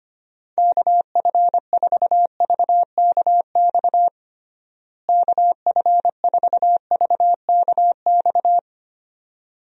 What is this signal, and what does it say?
Morse code, heard as a keyed tone, about 25 wpm, 700 Hz. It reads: KF4VKX KF4VKX
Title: USA Call Signs - 6 Characters - Repeated 2x - 25wpm